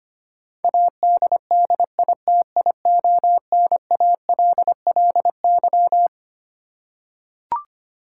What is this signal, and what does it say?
Morse code, heard as a keyed tone, about 25 wpm, 700 Hz. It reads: ADDITIONALLY
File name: Twelve Letter Words - 25wpm